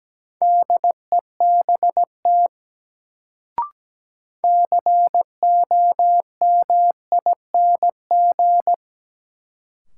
Morse code, heard as a keyed tone, about 17 wpm, 700 Hz. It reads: DEBT COMING
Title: Sentences from Top 1300 Words - New Word Drill Alt - 17wpm